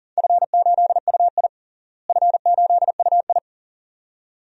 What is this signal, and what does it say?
Morse code, heard as a keyed tone, about 39 wpm, 690 Hz. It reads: F8UI F8UI
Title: DXCC-100 Call Signs - 4 Characters - Repeated 2x - 40wpm